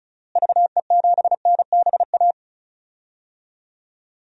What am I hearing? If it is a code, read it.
Morse code, heard as a keyed tone, about 34 wpm, 700 Hz. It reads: VE7NBA
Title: DXCC-100 Call Signs - 6 Characters - Repeated 2x - 35wpm